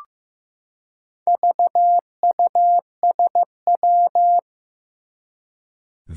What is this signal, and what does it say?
Morse code, heard as a keyed tone, about 15 wpm, 700 Hz. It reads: VUSW
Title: Four Challenging Letter-Number Combinations 15wpm